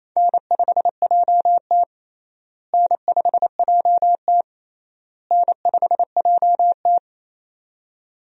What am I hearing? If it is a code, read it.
Morse code, heard as a keyed tone, about 28 wpm, 700 Hz. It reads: N5JT N5JT N5JT
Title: USA Call Signs - 4 Characters - Repeated 3x - 28wpm